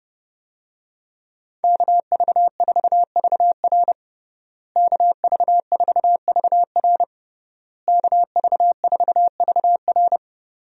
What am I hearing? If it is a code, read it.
Morse code, heard as a keyed tone, about 32 wpm, 700 Hz. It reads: KV4VR KV4VR KV4VR
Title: USA Call Signs - 5 Characters - Repeated 3x - 30wpm